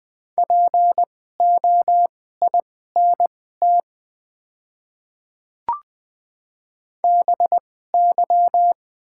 Morse code, heard as a keyed tone, about 20 wpm, spaced as for 14 wpm, 700 Hz. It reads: POINT BY
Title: 2x Character Spacing - Top 300 Words - 20wpm